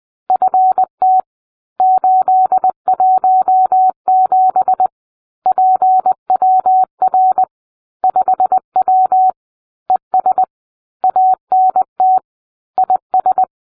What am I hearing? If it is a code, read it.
Morse code, heard as a keyed tone, about 20 wpm, 750 Hz. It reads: FT 817 PWR 5W ES ANT IS